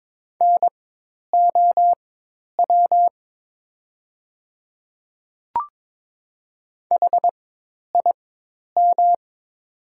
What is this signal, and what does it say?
Morse code, heard as a keyed tone, about 21 wpm, spaced as for 10 wpm, 700 Hz. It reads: NOW HIM
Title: 4x Character Spacing - Top 100 Words - 22wpm